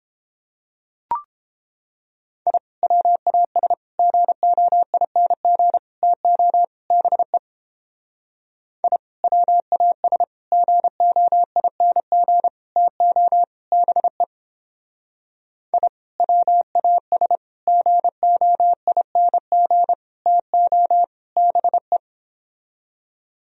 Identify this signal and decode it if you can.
Morse code, heard as a keyed tone, about 31 wpm, 700 Hz. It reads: I WAS GOING TO BE I WAS GOING TO BE I WAS GOING TO BE